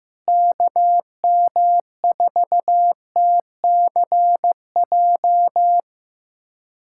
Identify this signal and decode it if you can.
Morse code, heard as a keyed tone, about 15 wpm, 700 Hz. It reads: KM4TCJ
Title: USA Callsign DE Callsign - 6 Characters - 15wpm